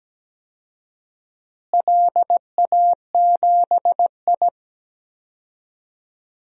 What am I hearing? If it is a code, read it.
Morse code, heard as a keyed tone, about 17 wpm, 700 Hz. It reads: LA7I
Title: DXCC-100 Call Signs - 4 Characters - Repeated 3x - 17wpm